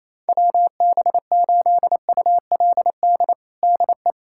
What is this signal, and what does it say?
Morse code, heard as a keyed tone, about 28 wpm, 700 Hz. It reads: WB8ULD DE
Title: USA Callsign DE Callsign - 6 Characters - 28wpm